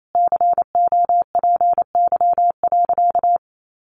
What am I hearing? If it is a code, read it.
Morse code, heard as a keyed tone, about 28 wpm, 700 Hz. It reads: COPY.
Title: Sentences from Top 1400 Words 28wpm